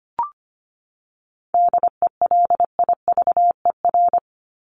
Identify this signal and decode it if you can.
Morse code, heard as a keyed tone, about 25 wpm, 700 Hz. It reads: DELIVER
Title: Top 1751-2000 Words 25wpm